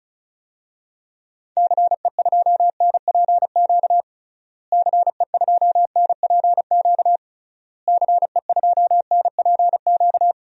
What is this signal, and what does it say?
Morse code, heard as a keyed tone, about 36 wpm, 700 Hz. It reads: CE2NPQ CE2NPQ CE2NPQ
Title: DXCC-100 Call Signs - 6 Characters - Repeated 3x - 35wpm